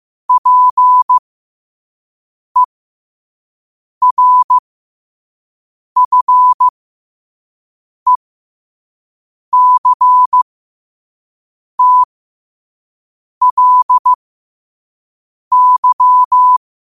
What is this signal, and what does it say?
Morse code, heard as a keyed tone, about 15 wpm, spaced as for 5 wpm, 1000 Hz. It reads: PERFECTLY
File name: qotdfiles / 5wpm